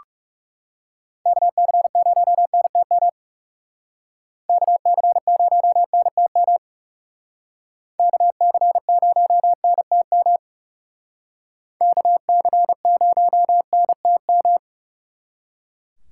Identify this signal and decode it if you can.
Morse code, heard as a keyed tone, about 34 wpm, 700 Hz. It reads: KC0NTM KC0NTM KC0NTM KC0NTM
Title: Speed Racing - USA 2x3 Call Signs 30wpm